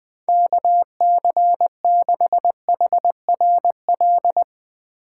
Morse code, heard as a keyed tone, about 20 wpm, 700 Hz. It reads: KC6HRL
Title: USA Call Signs - 2x Character Spacing - 20wpm